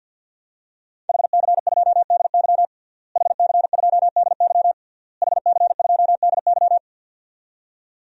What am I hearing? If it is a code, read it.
Morse code, heard as a keyed tone, about 48 wpm, 700 Hz. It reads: HC2DY HC2DY HC2DY
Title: DXCC-100 Call Signs - 5 Characters - Repeated 3x - 50wpm